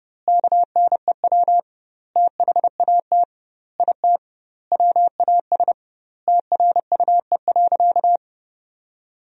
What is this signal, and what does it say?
Morse code, heard as a keyed tone, about 30 wpm, 700 Hz. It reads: KNEW THAT IT WAS TRUE.
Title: 2x Word-Spacing - Sentences from Top 400 Words - 30wpm